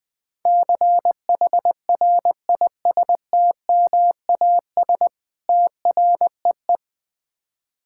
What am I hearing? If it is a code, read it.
Morse code, heard as a keyed tone, about 20 wpm, 700 Hz. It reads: CHRISTMAS TREE
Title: Difficult Words - Speed-Racing with Hints - 20wpm